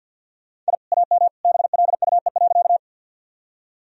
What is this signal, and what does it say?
Morse code, heard as a keyed tone, about 50 wpm, 700 Hz. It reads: I AM BLUE.
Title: Sentences from Top 1000 Words - Review Part 3 - 50wpm